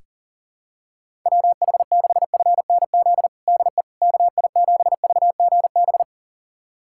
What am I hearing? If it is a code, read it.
Morse code, heard as a keyed tone, about 37 wpm, 700 Hz. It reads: WH6FNZ DE KI7VGB
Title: USA Callsign DE Callsign - 6 Characters - 40wpm